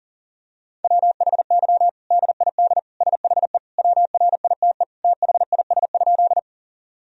Morse code, heard as a keyed tone, about 40 wpm, 700 Hz. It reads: WHY DID SHE WRITE THIS?
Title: Sentences from Top 1000 Words - Review Part 1 - 40wpm